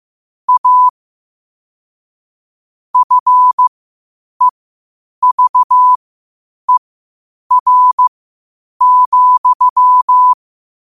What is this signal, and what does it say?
Morse code, heard as a keyed tone, about 15 wpm, spaced as for 8 wpm, 1000 Hz. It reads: A FEVER,
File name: qotdfiles / 8wpm